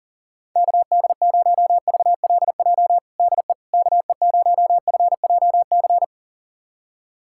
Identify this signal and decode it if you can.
Morse code, heard as a keyed tone, about 40 wpm, 700 Hz. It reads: KD0VLJ DE KE0FJC
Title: USA Callsign DE Callsign - 6 Characters - 40wpm